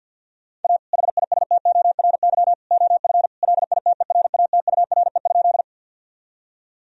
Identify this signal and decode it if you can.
Morse code, heard as a keyed tone, about 50 wpm, 700 Hz. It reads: A HISTORY OF LITERATURE?
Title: Sentences from Top 1000 Words - Review Part 3 - 50wpm